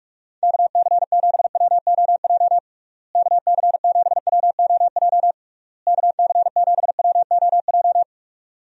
Morse code, heard as a keyed tone, about 41 wpm, 700 Hz. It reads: KC7WOJ KC7WOJ KC7WOJ
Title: USA Call Signs - 6 Characters - Repeated 3x - 45wpm